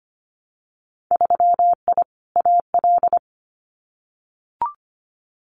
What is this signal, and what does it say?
Morse code, heard as a keyed tone, about 25 wpm, 700 Hz. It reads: 3I AL
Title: Winter Field Day Exchanges 25wpm